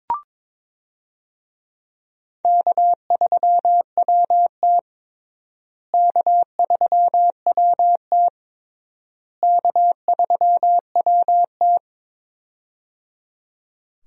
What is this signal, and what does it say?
Morse code, heard as a keyed tone, about 21 wpm, 700 Hz. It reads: K3WT K3WT K3WT
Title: USA Call Signs - 4 Characters - Repeated 3x - 22wpm